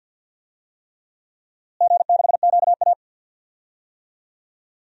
Morse code, heard as a keyed tone, about 53 wpm, 700 Hz. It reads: G6QA
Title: DXCC-100 Call Signs - 4 Characters - Repeated 3x - 50wpm